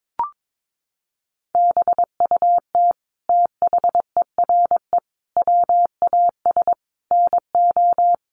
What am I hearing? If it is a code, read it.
Morse code, heard as a keyed tone, about 22 wpm, 700 Hz. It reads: BUT THERE WAS NO
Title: Sentences from Top 1400 Words 22wpm